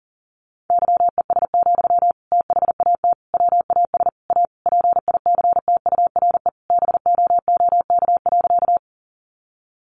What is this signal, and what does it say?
Morse code, heard as a keyed tone, about 40 wpm, 700 Hz. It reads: YES, THAT WAS A PICTURE BOOK.
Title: Sentences from Top 500 Words 40wpm